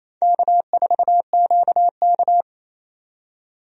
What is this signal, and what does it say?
Morse code, heard as a keyed tone, about 28 wpm, 700 Hz. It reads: K4QK
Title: USA Call Signs - 4 Characters - 28wpm